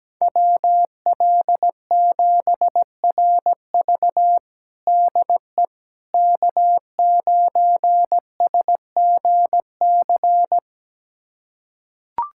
Morse code, heard as a keyed tone, about 17 wpm, 700 Hz. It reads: WL7RV DE K9SGC
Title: USA Callsign DE Callsign - 5 Characters - 17wpm